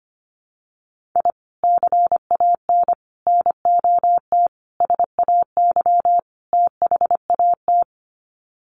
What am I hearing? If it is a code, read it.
Morse code, heard as a keyed tone, about 25 wpm, 700 Hz. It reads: I CAN NOT SAY THAT
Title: Sets of 5 Words Encore2 25wpm